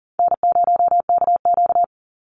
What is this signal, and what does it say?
Morse code, heard as a keyed tone, about 40 wpm, 700 Hz. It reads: N0KQ
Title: Easy Call Signs Repeated 3x - 40wpm